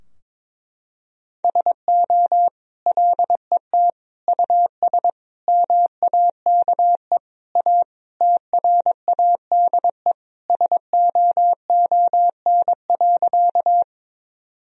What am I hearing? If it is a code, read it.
Morse code, heard as a keyed tone, about 22 wpm, 700 Hz. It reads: SO LET US MAKE A TRADE SOON.
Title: Sentences from Top 1000 Words - Review Part 4 - 22wpm